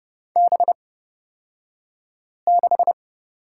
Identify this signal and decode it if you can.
Morse code, heard as a keyed tone, about 30 wpm, spaced as for 5 wpm, 700 Hz. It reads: B6
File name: Lesson 38d - Letter q - Two Characters 30wpm